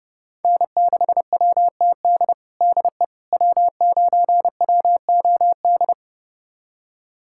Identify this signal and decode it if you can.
Morse code, heard as a keyed tone, about 31 wpm, 700 Hz. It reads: N6WTD DE W9WOD
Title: USA Callsign DE Callsign - 5 Characters - 30wpm